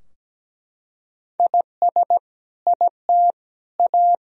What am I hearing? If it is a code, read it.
Morse code, heard as a keyed tone, about 17 wpm, 700 Hz. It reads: IS IT A
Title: Sentences from Top 1000 Words - Review Part 4 - 17wpm